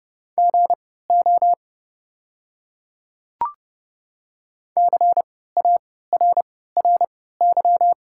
Morse code, heard as a keyed tone, about 31 wpm, spaced as for 17 wpm, 700 Hz. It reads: GO CARRY
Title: 3x Character Spacing - Top 500 Words - 30wpm